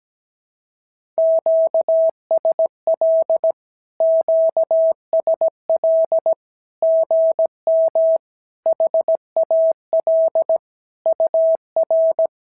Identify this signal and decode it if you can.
Morse code, heard as a keyed tone, about 17 wpm, 650 Hz. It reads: QSL QSL GM HAL UR
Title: SOTA QSO Practice 17wpm